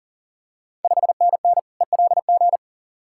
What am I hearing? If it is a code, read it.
Morse code, heard as a keyed tone, about 38 wpm, 690 Hz. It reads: 5NN ELG
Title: Ontario QSO Party Exchange 40wpm